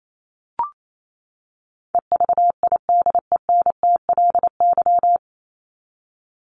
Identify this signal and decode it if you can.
Morse code, heard as a keyed tone, about 28 wpm, 700 Hz. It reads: EVIDENTLY
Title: LY Suffix - Top 100 Words with Base Words 28wpm